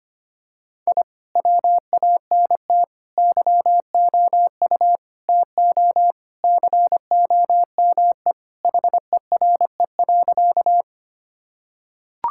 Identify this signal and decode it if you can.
Morse code, heard as a keyed tone, about 25 wpm, 700 Hz. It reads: I WANT YOU TO COME HERE.
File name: practice / 25wpm / Sentences with 2-Word Phrases - 25wpm